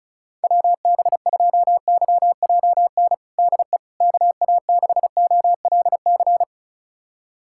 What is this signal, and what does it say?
Morse code, heard as a keyed tone, about 36 wpm, 700 Hz. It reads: WB2YJN DE KA6OLC
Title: USA Callsign DE Callsign - 6 Characters - 35wpm